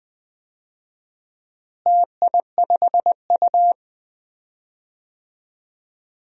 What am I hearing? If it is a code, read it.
Morse code, heard as a keyed tone, about 21 wpm, 700 Hz. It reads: TI5U
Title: DXCC-100 Call Signs - 4 Characters - Repeated 3x - 20wpm